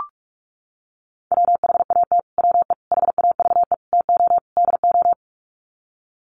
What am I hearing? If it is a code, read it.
Morse code, heard as a keyed tone, about 44 wpm, 700 Hz. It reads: WHAT WE HAVE TO DO
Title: Sets of 5 Words Encore 45wpm